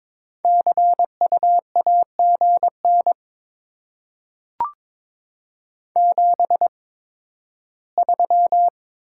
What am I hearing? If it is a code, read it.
Morse code, heard as a keyed tone, about 22 wpm, spaced as for 13 wpm, 700 Hz. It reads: CU AGN 73
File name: Top 20 QSO Elements - 8x Character Spacing - 22wpm